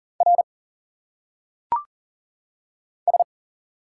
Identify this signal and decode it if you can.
Morse code, heard as a keyed tone, about 40 wpm, 700 Hz. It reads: R S
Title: Single Letters - Phonetic Pronunciation 40wpm